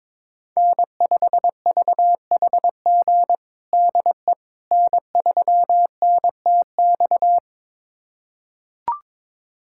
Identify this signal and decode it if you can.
Morse code, heard as a keyed tone, about 22 wpm, 700 Hz. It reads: N5VHG DE N3NTX
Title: USA Callsign DE Callsign - 5 Characters - 22wpm